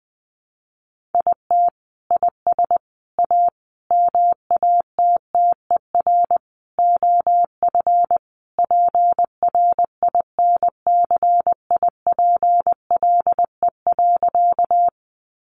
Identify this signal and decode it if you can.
Morse code, heard as a keyed tone, about 20 wpm, 700 Hz. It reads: IT IS A MATTER OF PRINCIPLE.
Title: Sentences from Top 1400 Words 20wpm